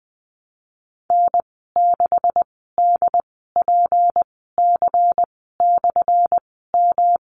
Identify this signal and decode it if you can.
Morse code, heard as a keyed tone, about 19 wpm, spaced as for 14 wpm, 700 Hz. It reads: N6DPC/M
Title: 179 Lesson 29g - Slash - Call Signs 20wpm